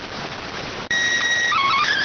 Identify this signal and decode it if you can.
SSTV (slow-scan television), PD 290.